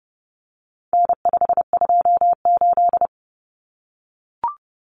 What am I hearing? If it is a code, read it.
Morse code, heard as a keyed tone, about 30 wpm, 700 Hz. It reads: N528
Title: Common Cut Numbers - 4-Digit 30wpm